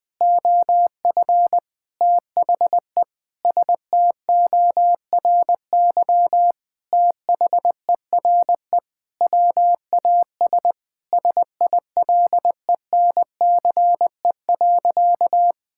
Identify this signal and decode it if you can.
Morse code, heard as a keyed tone, about 20 wpm, 700 Hz. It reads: OF THE STORY THERE WAS SILENCE.
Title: Sentences with 5-Word Phrases - 20wpm